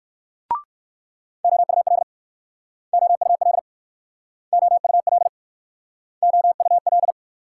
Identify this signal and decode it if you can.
Morse code, heard as a keyed tone, about 59 wpm, 700 Hz. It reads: OUL OUL OUL OUL